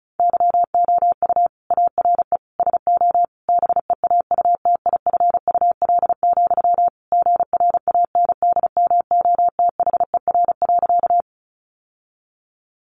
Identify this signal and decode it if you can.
Morse code, heard as a keyed tone, about 34 wpm, 700 Hz. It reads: YOU ARE SO BEAUTIFUL, GRANDMOTHER.
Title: Sentences from Top 1200 Words 35wpm